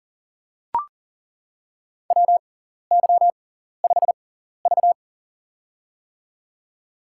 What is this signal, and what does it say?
Morse code, heard as a keyed tone, about 40 wpm, spaced as for 14 wpm, 700 Hz. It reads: WY5V
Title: USA Call Signs - 4 Characters - 6x Character Spacing - 40wpm